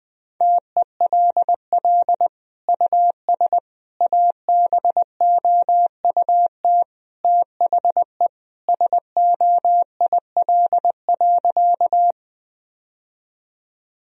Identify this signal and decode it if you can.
Morse code, heard as a keyed tone, about 20 wpm, 700 Hz. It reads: TELL US ABOUT THE SOIL.
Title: Sentences from Top 1000 Words - Review Part 4 - 20wpm